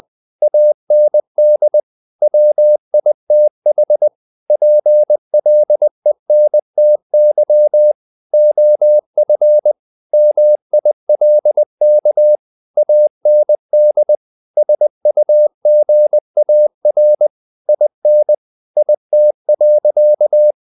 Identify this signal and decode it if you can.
Morse code, heard as a keyed tone, about 20 wpm, 600 Hz. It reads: AND WITH PLENTY OF MILK AND SUGAR IN IT.
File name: the old man and the sea part 2 20wpm